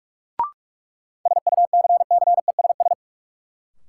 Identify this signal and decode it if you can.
Morse code, heard as a keyed tone, about 41 wpm, 700 Hz. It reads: SUCCESS
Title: Sentences from Top 900 Words - New Word Drill - 45wpm